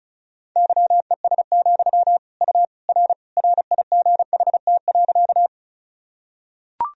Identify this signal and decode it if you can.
Morse code, heard as a keyed tone, about 36 wpm, 700 Hz. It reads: YES, U R RIGHT.
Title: Sentences from Top 500 Words and Abbreviations 35wpm